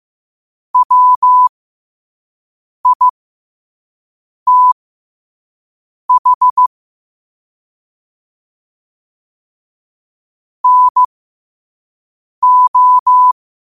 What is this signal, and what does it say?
Morse code, heard as a keyed tone, about 15 wpm, spaced as for 5 wpm, 1000 Hz. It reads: WITH NO
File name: qotdfiles / 5wpm